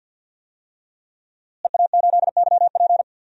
Morse code, heard as a keyed tone, about 53 wpm, 700 Hz. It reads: EA8YP
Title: DXCC-100 Call Signs - 5 Characters - Repeated 3x - 50wpm